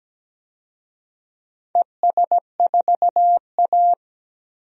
Morse code, heard as a keyed tone, about 17 wpm, 700 Hz. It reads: ES4A
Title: DXCC-100 Call Signs - 4 Characters - 17wpm